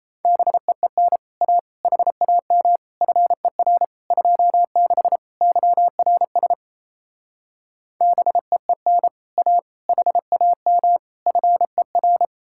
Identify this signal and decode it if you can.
Morse code, heard as a keyed tone, about 36 wpm, 700 Hz. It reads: BEEN A HAM FER 26 YRS BEEN A HAM FER